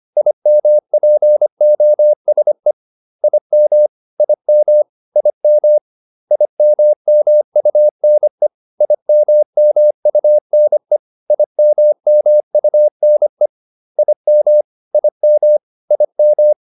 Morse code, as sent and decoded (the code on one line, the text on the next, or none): .. -- .--. --- ... / .. -- / .. -- / .. -- / .. -- -- ..- -. / .. -- -- ..- -. / .. -- -- ..- -. / .. -- / .. -- / .. --
IMPOSE IM IM IM IMMUNE IMMUNE IMMUNE IM IM IM